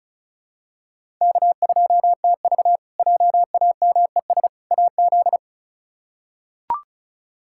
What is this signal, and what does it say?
Morse code, heard as a keyed tone, about 34 wpm, 700 Hz. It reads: K2TV JAMES AZ
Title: K1USN SST QSO Practice - 35wpm